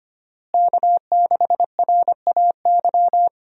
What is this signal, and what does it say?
Morse code, heard as a keyed tone, about 25 wpm, 700 Hz. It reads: K6RAY
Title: USA Call Signs - 5 Characters - Repeated 2x - 25wpm